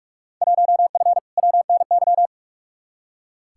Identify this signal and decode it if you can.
Morse code, heard as a keyed tone, about 42 wpm, 700 Hz. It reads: 1F WNY
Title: Field Day Exchanges - 45wpm